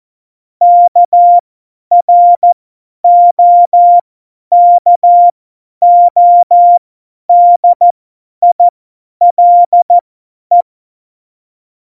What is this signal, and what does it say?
Morse code, heard as a keyed tone, about 14 wpm, spaced as for 10 wpm, 700 Hz. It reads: KROKODILE
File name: Morsekurs / Witze / Witz 09. Januar 2018, 50 F70